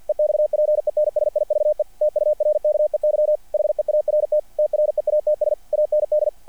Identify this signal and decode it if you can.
Morse code, heard as a keyed tone, about 50 wpm, 600 Hz. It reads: EXPENSIVE TURKEY HEART TREATS AND
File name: archived / CRSnet